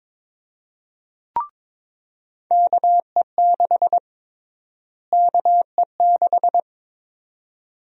Morse code, heard as a keyed tone, about 22 wpm, 700 Hz. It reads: KE6 KE6
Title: USA Call Sign Prefixes - Repeated 2x - 22wpm